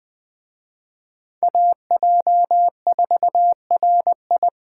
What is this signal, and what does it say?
Morse code, heard as a keyed tone, about 24 wpm, 700 Hz. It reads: AJ4RI
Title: Speed Racing - USA 2x2 Call Signs 20wpm